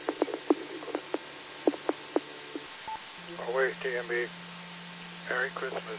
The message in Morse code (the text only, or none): none